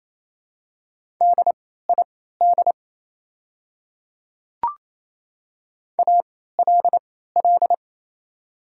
Morse code, heard as a keyed tone, about 28 wpm, spaced as for 16 wpm, 700 Hz. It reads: DID ALL
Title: 3x Character Spacing - Top 100 Words - 28wpm